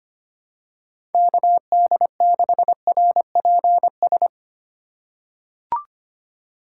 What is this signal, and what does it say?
Morse code, heard as a keyed tone, about 25 wpm, 700 Hz. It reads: KD6RPS
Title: USA Call Signs - 6 Characters - 4x Character Spacing - 25wpm